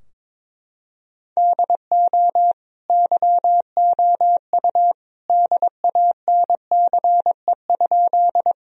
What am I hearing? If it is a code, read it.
Morse code, heard as a keyed tone, about 22 wpm, 700 Hz. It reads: DO YOU DANCE?
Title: Sentences from Top 1000 Words - Review Part 4 - 22wpm